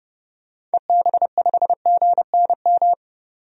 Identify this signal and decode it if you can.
Morse code, heard as a keyed tone, about 30 wpm, 700 Hz. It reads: EB5GNM
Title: DXCC-100 Call Signs - 6 Characters - 30wpm